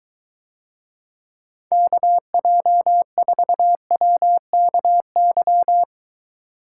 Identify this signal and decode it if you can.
Morse code, heard as a keyed tone, about 23 wpm, 700 Hz. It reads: KJ4WKY